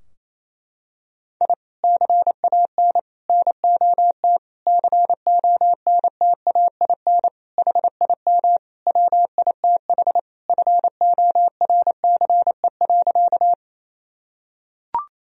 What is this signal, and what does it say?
Morse code, heard as a keyed tone, about 28 wpm, 700 Hz. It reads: I CAN NOT CONTAIN HIM WITH FORCE.
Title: Sentences from Top 1000 Words - Review Part 3 - 28wpm